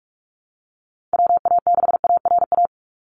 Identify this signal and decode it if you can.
Morse code, heard as a keyed tone, about 46 wpm, 700 Hz. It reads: WA6ARA
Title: NA SOTA Chaser Callsigns 45wpm